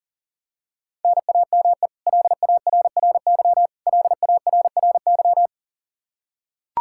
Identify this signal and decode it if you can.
Morse code, heard as a keyed tone, about 40 wpm, 700 Hz. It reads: NAME LARRY LARRY
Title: Practice QSO - Exchange 1 - Beginning - 40wpm